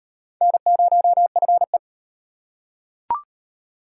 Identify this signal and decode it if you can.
Morse code, heard as a keyed tone, about 38 wpm, 700 Hz. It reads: N0FE